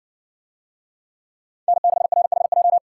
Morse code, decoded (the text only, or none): N6RHJ